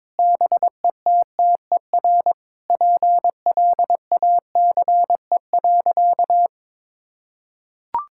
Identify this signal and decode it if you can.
Morse code, heard as a keyed tone, about 22 wpm, 700 Hz. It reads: BETTER PLACE.
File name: Sentences with 7-Word Phrases - 22wpm